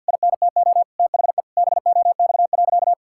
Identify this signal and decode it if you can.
Morse code, heard as a keyed tone, about 49 wpm, 700 Hz. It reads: INTO THE BOX.
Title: Sentences from Top 1000 Words - Review Part 6 - 50wpm